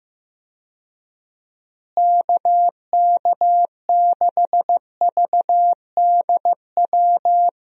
Morse code, heard as a keyed tone, about 15 wpm, 700 Hz. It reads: KK6VDW
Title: USA Call Signs - 6 Characters - Repeated 3x - 15wpm